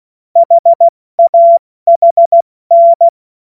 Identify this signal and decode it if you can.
Morse code, heard as a keyed tone, about 16 wpm, 680 Hz. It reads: HAHN
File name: Witz-08-Juli-2025-70-F80